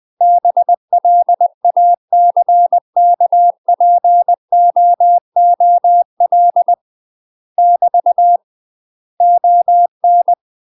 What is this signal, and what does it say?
Morse code, heard as a keyed tone, about 20 wpm, 700 Hz. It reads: BLACKPOOL <BT> ON
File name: UK MASTER QSOs TRAINING / 20WPM / UK MASTER QSOs 0063